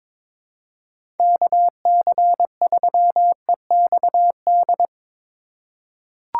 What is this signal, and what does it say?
Morse code, heard as a keyed tone, about 22 wpm, 700 Hz. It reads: KC3EXD
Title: USA Call Signs - 3x Character Spacing - 22wpm